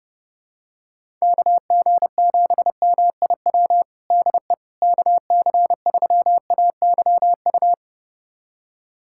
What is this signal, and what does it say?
Morse code, heard as a keyed tone, about 30 wpm, 700 Hz. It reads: KG7MIW DE KC3AYU
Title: USA Callsign DE Callsign 30wpm